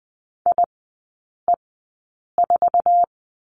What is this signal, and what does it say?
Morse code, heard as a keyed tone, about 20 wpm, spaced as for 8 wpm, 700 Hz. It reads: IE4
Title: 074 Lesson 13e - Letter L - Three Characters 20wpm